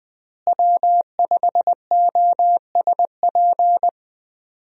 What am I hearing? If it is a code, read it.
Morse code, heard as a keyed tone, about 20 wpm, 700 Hz. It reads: W5OSP
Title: USA Call Signs - 5 Characters - Repeated 3x - 20wpm